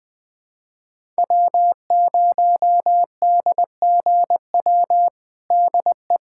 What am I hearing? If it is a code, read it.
Morse code, heard as a keyed tone, about 20 wpm, 700 Hz. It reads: W0DGW DE
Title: USA Callsign DE Callsign - 5 Characters - 20wpm